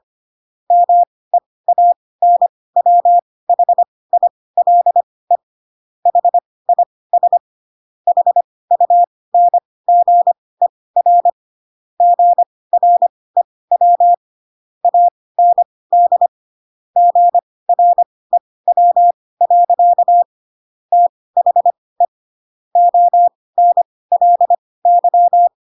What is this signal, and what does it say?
Morse code, heard as a keyed tone, about 25 wpm, spaced as for 18 wpm, 700 Hz. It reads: MEANWHILE HIS HUNGER GREW AND GREW. THE ONLY
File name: Pinocchio / Pinocchio ch5 12 25 ik1hge